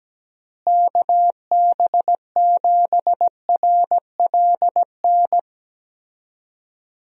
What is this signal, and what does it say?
Morse code, heard as a keyed tone, about 17 wpm, 700 Hz. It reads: KB7RLN
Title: USA Call Signs - 6 Characters - Repeated 2x - 17wpm